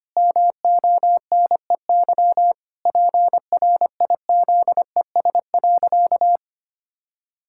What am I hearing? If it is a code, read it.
Morse code, heard as a keyed tone, about 25 wpm, 700 Hz. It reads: MONEY PRIZES.
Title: Pangram Challenge v2 - 25wpm